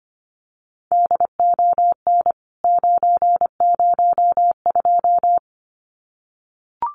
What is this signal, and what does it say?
Morse code, heard as a keyed tone, about 25 wpm, 690 Hz. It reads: DON 902
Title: CWT Exchange 25wpm